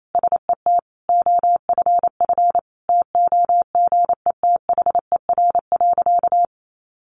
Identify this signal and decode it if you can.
Morse code, heard as a keyed tone, about 28 wpm, 700 Hz. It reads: SET OFF TOGETHER.
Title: Sentences from Top 1500 Words 28wpm